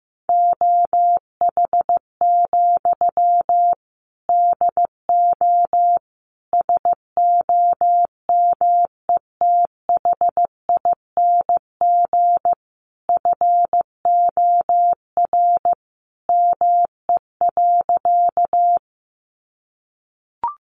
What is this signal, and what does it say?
Morse code, heard as a keyed tone, about 15 wpm, 700 Hz. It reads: OH, DO SOMETHING FOR ME.
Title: Sentences from Top 600 Words 15wpm